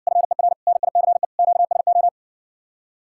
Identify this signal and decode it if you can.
Morse code, heard as a keyed tone, about 60 wpm, 700 Hz. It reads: FER NICE QSO